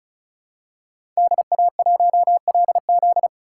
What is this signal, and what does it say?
Morse code, heard as a keyed tone, about 36 wpm, 700 Hz. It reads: DA1LZ
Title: USA Call Signs - 5 Characters - 2x Character Spacing - 35wpm